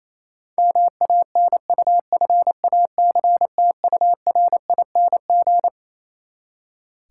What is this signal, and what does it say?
Morse code, heard as a keyed tone, about 28 wpm, 700 Hz. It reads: MANUFACTURING
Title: Practice QSO - Exchange 3 - More Info - 28wpm